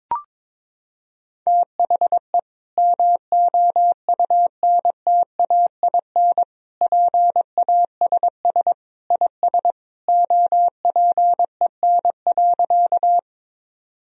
Morse code, heard as a keyed tone, about 22 wpm, 700 Hz. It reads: THE MOUNTAIN PASS IS OPEN.
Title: Sentences from Top 1000 Words - Review Part 1 - 22wpm